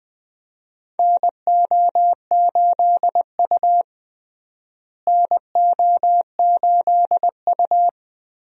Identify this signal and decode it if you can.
Morse code, heard as a keyed tone, about 20 wpm, 700 Hz. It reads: NO8U NO8U
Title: USA Call Signs - 4 Characters - Repeated 3x - 20wpm